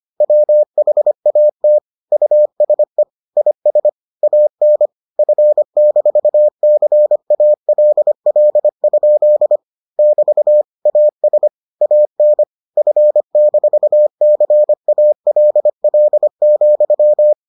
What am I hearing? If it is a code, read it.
Morse code, heard as a keyed tone, about 25 wpm, 600 Hz. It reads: WHAT USE IS AN F-CALL? <BT> AS AN F-CALL,